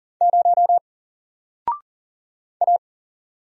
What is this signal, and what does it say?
Morse code, heard as a keyed tone, about 40 wpm, 700 Hz. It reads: 0 A
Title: Single Letter-Number 40wpm